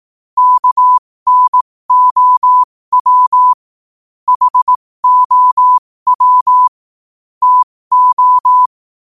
Morse code, heard as a keyed tone, about 18 wpm, spaced as for 15 wpm, 1000 Hz. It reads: KNOW HOW TO
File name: qotdfiles / 15wpm